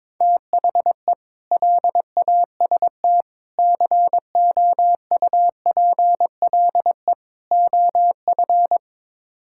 Morse code, heard as a keyed tone, about 22 wpm, 700 Hz. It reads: THE LAST COUPLE OF
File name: practice / 22wpm / Build A Phrase - 5-Word Phrases - 22wpm